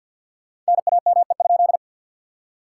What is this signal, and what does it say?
Morse code, heard as a keyed tone, about 68 wpm, spaced as for 58 wpm, 700 Hz. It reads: NAME?
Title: Speed-Racing Common QSO Phrases - 50wpm